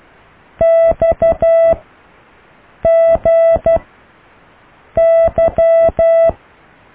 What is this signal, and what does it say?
Morse code, heard as a keyed tone, about 12 wpm, 660 Hz. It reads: X G Y